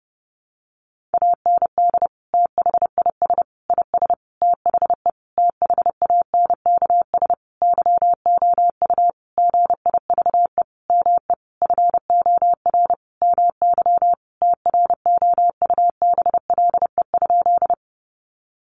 Morse code, heard as a keyed tone, about 30 wpm, 700 Hz. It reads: AND THIS IS THE THANKS YOU GIVE ME FOR MY TROUBLE?
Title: Sentences from Top 1200 Words 30wpm